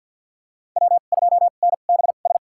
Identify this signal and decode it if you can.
Morse code, heard as a keyed tone, about 48 wpm, spaced as for 35 wpm, 700 Hz. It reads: W2NBS
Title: USA Call Signs - 5 Characters - 2x Character Spacing - 50wpm